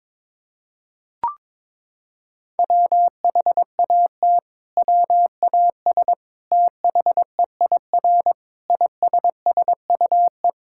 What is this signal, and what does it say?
Morse code, heard as a keyed tone, about 22 wpm, 700 Hz. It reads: WHAT WAS THEIR ISSUE
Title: Sentences from Top 1000 Words - Review Part 6 - 22wpm